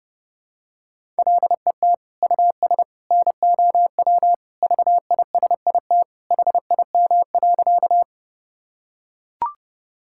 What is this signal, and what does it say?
Morse code, heard as a keyed tone, about 30 wpm, 700 Hz. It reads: LET US NOW VISIT HIM.
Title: Sentences from Top 1000 Words - Review Part 6 - 30wpm